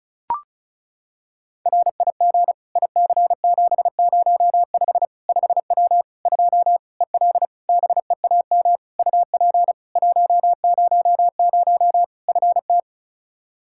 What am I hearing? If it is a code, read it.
Morse code, heard as a keyed tone, about 36 wpm, 700 Hz. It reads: RIG IC705 5W 2 EL BEAM UP 100 FT
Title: K1USN SST QSO Practice 35wpm